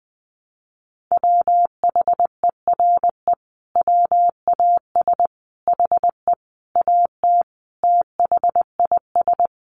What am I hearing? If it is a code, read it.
Morse code, heard as a keyed tone, about 20 wpm, 700 Hz. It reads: WHERE WAS HE AT THIS
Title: Sentences from Top 1000 Words 20wpm